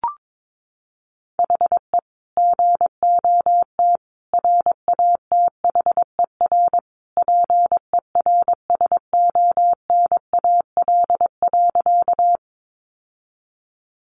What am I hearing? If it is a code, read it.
Morse code, heard as a keyed tone, about 22 wpm, 700 Hz. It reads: HE GOT RATHER PERSONAL.
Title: Sentences from Top 700 Words 22wpm